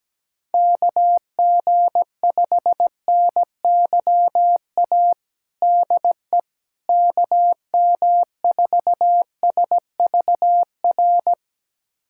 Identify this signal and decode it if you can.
Morse code, heard as a keyed tone, about 17 wpm, 700 Hz. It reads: KG5NYA DE KM4SVR
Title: USA Callsign DE Callsign - 6 Characters - 17wpm